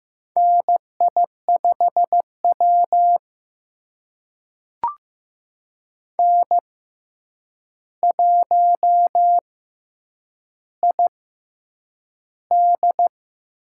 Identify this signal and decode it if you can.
Morse code, heard as a keyed tone, about 15 wpm, spaced as for 8 wpm, 700 Hz. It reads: NI5W N1ID
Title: USA Call Signs - 6x Character Spacing - 15wpm